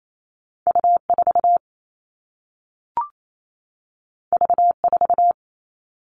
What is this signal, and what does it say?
Morse code, heard as a keyed tone, about 28 wpm, 700 Hz. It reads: U4 V4
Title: All Cut Numbers - 2-Digit 28wpm